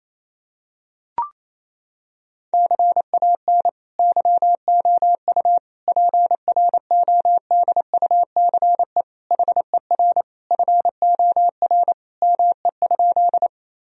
Morse code, heard as a keyed tone, about 28 wpm, 700 Hz. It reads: CAN YOU PRODUCE HER FOR ME?
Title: Sentences from Top 1000 Words - Review Part 3 - 28wpm